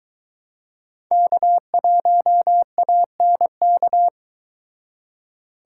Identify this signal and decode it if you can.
Morse code, heard as a keyed tone, about 22 wpm, 700 Hz. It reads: K1ANK